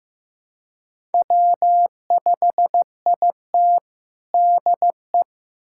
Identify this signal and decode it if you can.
Morse code, heard as a keyed tone, about 15 wpm, 700 Hz. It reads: W5IT DE
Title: USA Callsign DE Callsign - 4 Characters - 15wpm